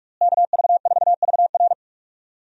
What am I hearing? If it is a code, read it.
Morse code, heard as a keyed tone, about 46 wpm, 700 Hz. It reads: KV4VR
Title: USA Call Signs - 5 Characters - Repeated 3x - 45wpm